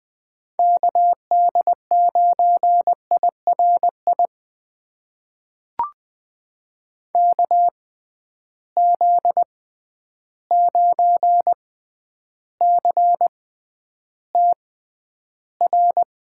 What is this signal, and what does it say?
Morse code, heard as a keyed tone, about 20 wpm, spaced as for 10 wpm, 700 Hz. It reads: KD9IRI KZ9CTR
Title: USA Call Signs - 6 Characters - 6x Character Spacing - 20wpm